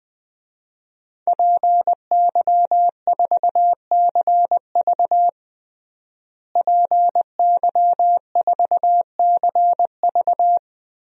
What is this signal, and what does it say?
Morse code, heard as a keyed tone, about 21 wpm, 700 Hz. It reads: PY4CV PY4CV
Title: DXCC-100 Call Signs - 5 Characters - Repeated 2x - 20wpm